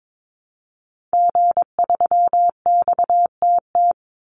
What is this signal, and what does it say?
Morse code, heard as a keyed tone, about 22 wpm, 700 Hz. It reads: G3XTT
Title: Call Signs Repeated 2x - 22wpm